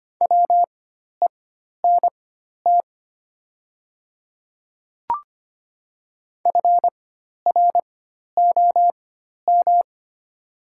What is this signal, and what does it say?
Morse code, heard as a keyed tone, about 25 wpm, spaced as for 12 wpm, 700 Hz. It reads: WENT FROM
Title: 4x Character Spacing - Top 200 Words - 25wpm